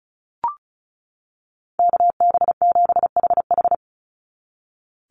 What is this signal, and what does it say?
Morse code, heard as a keyed tone, about 34 wpm, 700 Hz. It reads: KB7HH
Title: NA SOTA Chaser Callsigns 35wpm